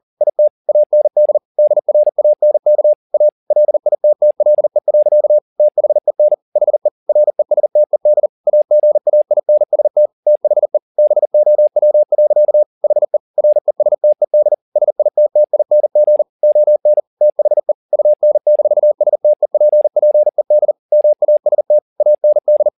Morse code, heard as a keyed tone, about 40 wpm, 600 Hz. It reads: IT AND DRANK A LITTLE. THEN HE RESTED AGAINST THE BOW. HE RESTED SITTING ON THE UN-STEPPED MAST AND